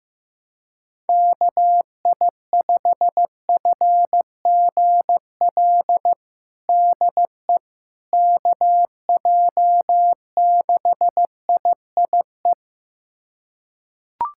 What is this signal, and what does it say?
Morse code, heard as a keyed tone, about 15 wpm, 700 Hz. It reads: KI5FGL DE KJ6IIE
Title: USA Callsign DE Callsign - 6 Characters - 15wpm